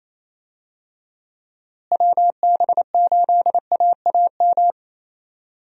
Morse code, heard as a keyed tone, about 28 wpm, 700 Hz. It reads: WB8AAM
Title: USA Call Signs - 6 Characters - Repeated 2x - 28wpm